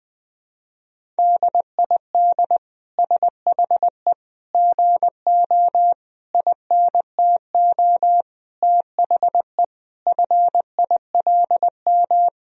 Morse code, heard as a keyed tone, about 20 wpm, 700 Hz. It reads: DID SHE GO INTO THE FILM
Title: Sentences from Top 1000 Words - Review Part 3 - 20wpm